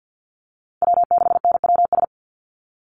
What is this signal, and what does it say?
Morse code, heard as a keyed tone, about 50 wpm, 700 Hz. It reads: W6NWS
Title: Call Signs - 50wpm